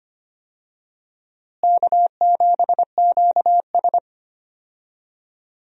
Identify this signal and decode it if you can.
Morse code, heard as a keyed tone, about 25 wpm, 700 Hz. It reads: K7QS
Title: USA Call Signs - 4 Characters - Repeated 3x - 25wpm